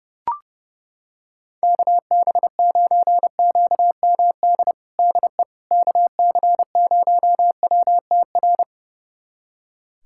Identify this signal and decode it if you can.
Morse code, heard as a keyed tone, about 30 wpm, 700 Hz. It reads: KB9QMD DE KC0WTR
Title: USA Callsign DE Callsign 30wpm